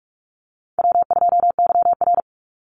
Morse code, heard as a keyed tone, about 44 wpm, 700 Hz. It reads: W2YR
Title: Call Signs Repeated 2x - 45wpm